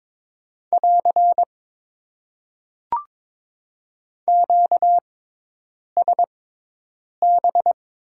Morse code, heard as a keyed tone, about 22 wpm, spaced as for 8 wpm, 700 Hz. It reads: <AR> QSB
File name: Top 90 QSO Elements - 6x Character Spacing - 22wpm